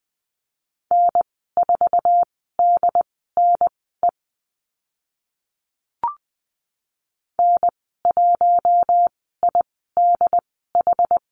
Morse code, heard as a keyed tone, about 19 wpm, spaced as for 14 wpm, 700 Hz. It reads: N4DNE N1IDH
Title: 069 Lesson 12g - Letter d - Call Signs 20wpm